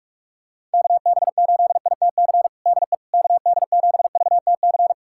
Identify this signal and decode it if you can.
Morse code, heard as a keyed tone, about 44 wpm, 700 Hz. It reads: KB8ITC DE KD7VTC
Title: USA Callsign DE Callsign - 6 Characters - 45wpm